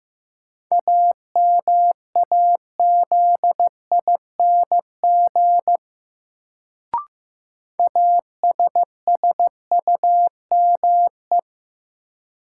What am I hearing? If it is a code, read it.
Morse code, heard as a keyed tone, about 15 wpm, 700 Hz. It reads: AMAZING ASSUME
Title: Sentences from Top 1500 Words - New Word Drill - 15wpm